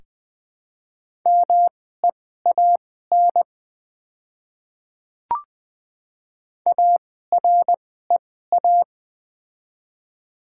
Morse code, heard as a keyed tone, about 20 wpm, spaced as for 14 wpm, 700 Hz. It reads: MEAN AREA
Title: 2x Character Spacing - Top 500 Words - 20wpm